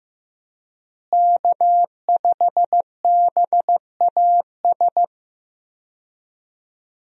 Morse code, heard as a keyed tone, about 15 wpm, 700 Hz. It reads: K5BAS
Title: USA Call Signs - 5 Characters - Repeated 3x - 15wpm